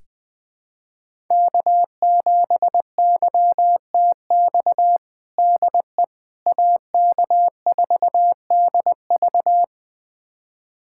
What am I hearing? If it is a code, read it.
Morse code, heard as a keyed tone, about 20 wpm, 700 Hz. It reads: K7YTX DE AK4DV
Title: USA Callsign DE Callsign - 5 Characters - 20wpm